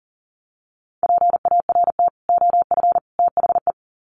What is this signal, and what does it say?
Morse code, heard as a keyed tone, about 40 wpm, 700 Hz. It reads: PART OF THE
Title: Sets of 3 Words 40wpm